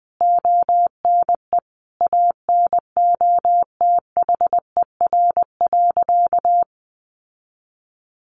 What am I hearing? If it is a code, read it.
Morse code, heard as a keyed tone, about 20 wpm, 700 Hz. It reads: ONE ANOTHER.
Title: Sentences from Top 1400 Words 20wpm